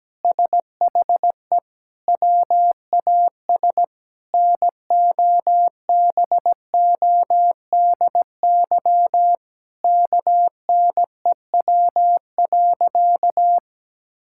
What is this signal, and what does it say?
Morse code, heard as a keyed tone, about 17 wpm, 700 Hz. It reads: SHE WAS NOBODY KNEW.
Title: Sentences from Top 1600 Words 17wpm